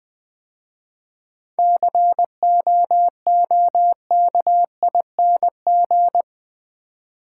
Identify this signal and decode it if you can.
Morse code, heard as a keyed tone, about 20 wpm, 700 Hz. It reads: COOKING
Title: 7 Letter Words - First 3 Letters Given - 20wpm